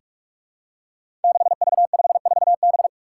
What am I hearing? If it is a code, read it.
Morse code, heard as a keyed tone, about 44 wpm, 700 Hz. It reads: 6V54B
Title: Five Challenging Letter-Number Combinations 45wpm